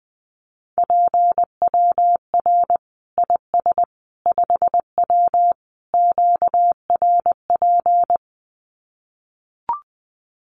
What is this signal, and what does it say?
Morse code, heard as a keyed tone, about 20 wpm, 700 Hz. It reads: PWR IS 5W QRP
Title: Common QSO Phrases 20wpm